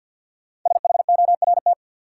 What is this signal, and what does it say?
Morse code, heard as a keyed tone, about 53 wpm, 700 Hz. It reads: SHORT
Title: Sentences from Top 400 Words - New Word Drill - 50wpm